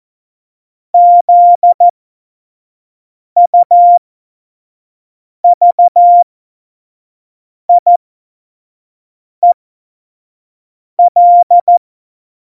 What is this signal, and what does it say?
Morse code, heard as a keyed tone, about 14 wpm, spaced as for 5 wpm, 700 Hz. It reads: ZUVIEL
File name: Morsekurs / Witze / Witz 08. August 17, Tempo 25 F70